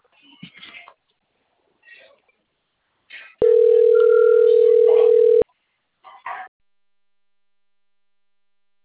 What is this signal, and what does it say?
Ringback tone (North America), 440 + 480 Hz.